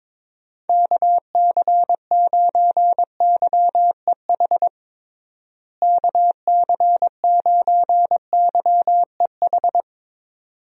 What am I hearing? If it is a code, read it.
Morse code, heard as a keyed tone, about 22 wpm, 700 Hz. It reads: KC9YEH KC9YEH
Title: USA Call Signs - 6 Characters - Repeated 2x - 22wpm